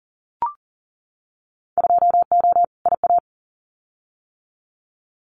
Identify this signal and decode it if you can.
Morse code, heard as a keyed tone, about 40 wpm, 700 Hz. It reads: 2O IA
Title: Winter Field Day Exchanges 40wpm